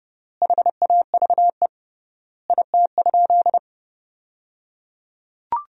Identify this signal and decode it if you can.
Morse code, heard as a keyed tone, about 30 wpm, 700 Hz. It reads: HAVE IT?
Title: 3x Word-Spacing - Sentences from Top 100 Words - 30wpm